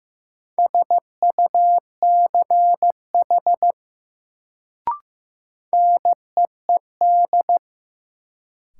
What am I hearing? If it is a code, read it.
Morse code, heard as a keyed tone, about 15 wpm, 700 Hz. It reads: SUCH NEED
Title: Sentences from Top 200 Words - New Word Drill - 15wpm